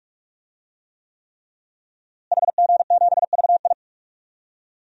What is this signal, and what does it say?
Morse code, heard as a keyed tone, about 41 wpm, 700 Hz. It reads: HG7VI
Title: DXCC-100 Call Signs - 5 Characters - Repeated 3x - 45wpm